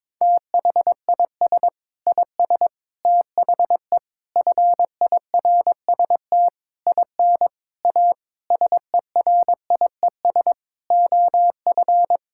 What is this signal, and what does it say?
Morse code, heard as a keyed tone, about 22 wpm, 700 Hz. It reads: THIS IS THE FIRST IN A SERIES OF
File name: practice / 22wpm / Sentences with 8-Word Phrases - 22wpm